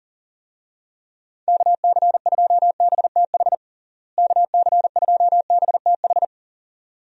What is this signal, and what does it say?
Morse code, heard as a keyed tone, about 37 wpm, 700 Hz. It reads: KC2BTH KC2BTH
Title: USA Call Signs - 6 Characters - Repeated 2x - 40wpm